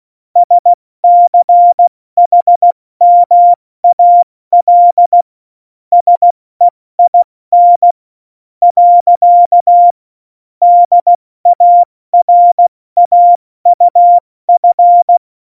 Morse code, heard as a keyed tone, about 16 wpm, 700 Hz. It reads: SCHMAL SEIN . DARAUF
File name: Morsekurs / Witze / Witz 25. Dezember, 70 F80